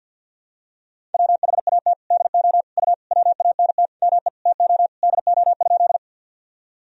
Morse code, heard as a keyed tone, about 48 wpm, 700 Hz. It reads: WHAT DO U WANT ME TO DO?
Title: Sentences from Top 200 Words and Abbreviations 50wpm